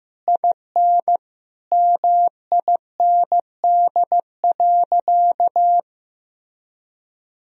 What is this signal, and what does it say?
Morse code, heard as a keyed tone, about 15 wpm, 700 Hz. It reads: IN MIND.
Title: Sentences from Top 1000 Words - Review Part 3 - 15wpm